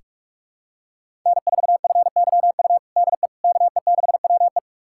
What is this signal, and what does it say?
Morse code, heard as a keyed tone, about 46 wpm, 700 Hz. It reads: N4FYU DE KE6WE
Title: USA Callsign DE Callsign - 5 Characters - 45wpm